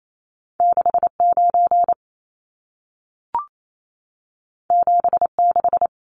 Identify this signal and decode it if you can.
Morse code, heard as a keyed tone, about 28 wpm, 700 Hz. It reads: 69 76
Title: Numbers - 2-Digit 28wpm